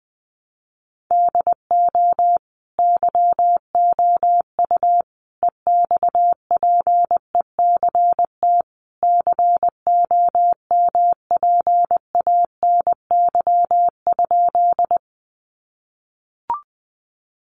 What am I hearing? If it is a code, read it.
Morse code, heard as a keyed tone, about 20 wpm, 700 Hz. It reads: DO YOU EXPECT COMPANY?
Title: Sentences from Top 1100 Words 20wpm